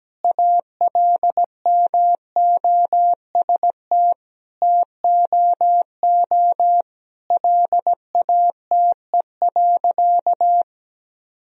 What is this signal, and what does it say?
Morse code, heard as a keyed tone, about 17 wpm, 700 Hz. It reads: ALMOST TOO LATE.
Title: Sentences from Top 1000 Words - Review Part 4 - 17wpm